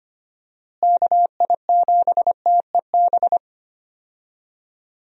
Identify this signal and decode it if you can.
Morse code, heard as a keyed tone, about 25 wpm, 700 Hz. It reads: KI7TEB
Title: USA Call Signs - 6 Characters - Repeated 3x - 25wpm